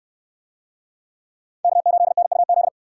Morse code, heard as a keyed tone, about 60 wpm, 700 Hz. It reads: K8NUZ